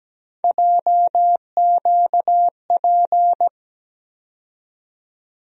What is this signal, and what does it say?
Morse code, heard as a keyed tone, about 17 wpm, 700 Hz. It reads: JQP
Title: USA Call Sign Postfixes - 3 Letters 17wpm